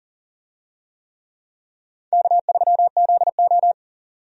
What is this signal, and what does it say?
Morse code, heard as a keyed tone, about 40 wpm, 700 Hz. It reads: K3ZO
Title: USA Call Signs - 4 Characters - 40wpm